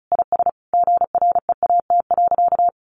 Morse code, heard as a keyed tone, about 34 wpm, 700 Hz. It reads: IS GREAT.
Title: Sentences from Top 1100 Words 35wpm